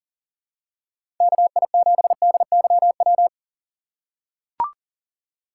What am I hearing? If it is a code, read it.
Morse code, heard as a keyed tone, about 40 wpm, 700 Hz. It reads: KI7DYW
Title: USA Call Signs - 4x Character Spacing - 40wpm